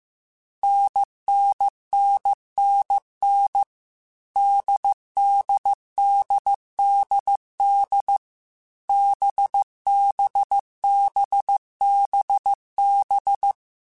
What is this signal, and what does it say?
Morse code, heard as a keyed tone, about 15 wpm, 800 Hz. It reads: NNNNN DDDDD BBBBB
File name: Difficult Combos / Difficult Combos-mp3s / NDB NDB6 @13